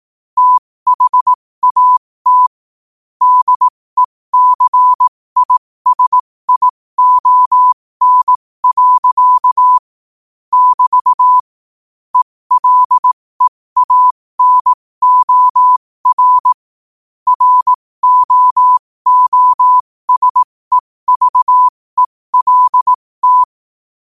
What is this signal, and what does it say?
Morse code, heard as a keyed tone, about 18 wpm, spaced as for 15 wpm, 1000 Hz. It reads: THAT DECISION. <BT> ELEANOR ROOSEVELT
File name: qotdfiles / 15wpm